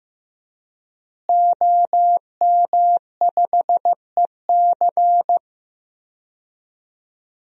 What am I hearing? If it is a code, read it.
Morse code, heard as a keyed tone, about 16 wpm, 700 Hz. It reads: OM5EC
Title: DXCC-100 Call Signs - 5 Characters - Repeated 3x - 15wpm